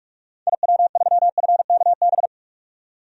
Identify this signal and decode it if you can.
Morse code, heard as a keyed tone, about 41 wpm, 700 Hz. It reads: IW3FKB
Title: DXCC-100 Call Signs - 6 Characters - 45wpm